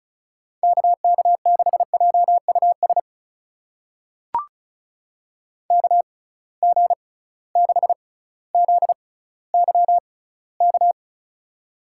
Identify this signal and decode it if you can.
Morse code, heard as a keyed tone, about 34 wpm, spaced as for 18 wpm, 700 Hz. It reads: KK6JUS KG6ZYK
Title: USA Call Signs - 6 Characters - 6x Character Spacing - 35wpm